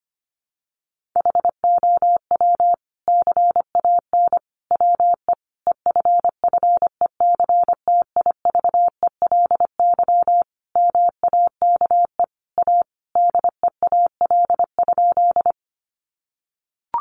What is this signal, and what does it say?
Morse code, heard as a keyed tone, about 25 wpm, 700 Hz. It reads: HOW CAN WE EFFECTIVELY MAKE A DEAL?
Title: Sentences from Top 1400 Words 25wpm